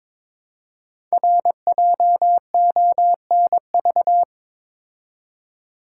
Morse code, heard as a keyed tone, about 21 wpm, 700 Hz. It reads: RJONV
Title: Code Groups - 5 Letters - 22wpm